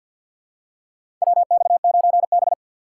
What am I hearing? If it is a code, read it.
Morse code, heard as a keyed tone, about 53 wpm, 700 Hz. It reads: WX9B
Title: USA Call Signs - 4 Characters - 6x Character Spacing - 50wpm